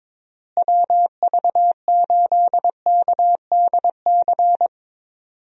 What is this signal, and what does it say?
Morse code, heard as a keyed tone, about 22 wpm, 700 Hz. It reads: WV8KDC
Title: USA Call Signs - 6 Characters - Repeated 2x - 22wpm